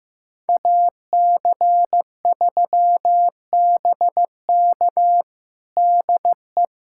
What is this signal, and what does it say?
Morse code, heard as a keyed tone, about 16 wpm, 700 Hz. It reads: AC3BK DE
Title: USA Callsign DE Callsign - 5 Characters - 15wpm